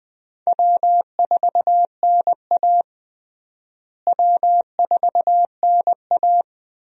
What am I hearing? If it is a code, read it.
Morse code, heard as a keyed tone, about 21 wpm, 700 Hz. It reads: W4NA W4NA
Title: USA Call Signs - 4 Characters - Repeated 2x - 20wpm